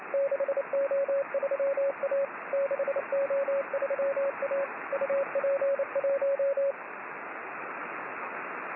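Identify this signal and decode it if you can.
Morse code, heard as a keyed tone, about 28 wpm, 550 Hz, with radio noise. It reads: 6O3A 6O3A UP1
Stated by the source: Morse signal on the 6 m band.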